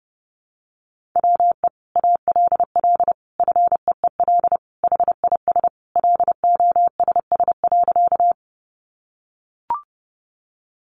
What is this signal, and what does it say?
Morse code, heard as a keyed tone, about 30 wpm, 700 Hz. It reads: WE ALL FEEL HIS LOSS.
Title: Sentences from Top 1400 Words 30wpm